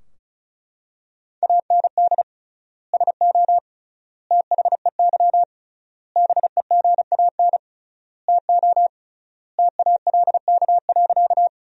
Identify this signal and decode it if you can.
Morse code, heard as a keyed tone, about 36 wpm, 700 Hz. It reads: AND SO THEY BEGAN TO TALK.
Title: 3x Word-Spacing - Sentences from Top 400 Words - 35wpm